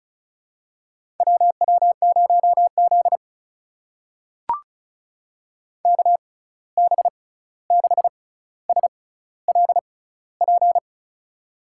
Morse code, heard as a keyed tone, about 36 wpm, spaced as for 16 wpm, 700 Hz. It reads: WW0Z KB6SLP
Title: USA Call Signs - 6x Character Spacing - 35wpm